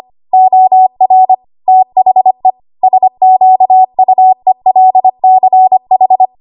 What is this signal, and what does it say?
Morse code, heard as a keyed tone, about 25 wpm, 750 Hz. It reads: OR THE SQUELCH
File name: Archive / 25 WPM / 25WPM